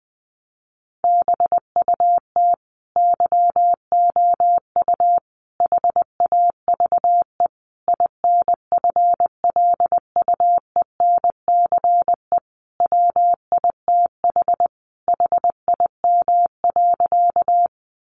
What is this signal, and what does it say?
Morse code, heard as a keyed tone, about 20 wpm, 700 Hz. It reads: BUT YOU HAVE INFLUENCE WITH HIM.
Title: Sentences from Top 1100 Words 20wpm